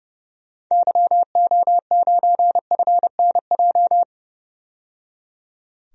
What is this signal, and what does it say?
Morse code, heard as a keyed tone, about 31 wpm, 700 Hz. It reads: YO9FNJ
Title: DXCC-100 Call Signs - 6 Characters - 30wpm